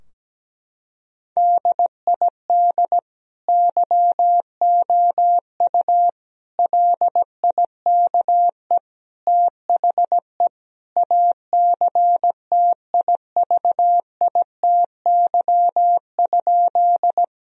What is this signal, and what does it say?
Morse code, heard as a keyed tone, about 17 wpm, 700 Hz. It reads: DID YOU LIKE THE ACTIVITY?
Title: Sentences from Top 1000 Words - Review Part 4 - 17wpm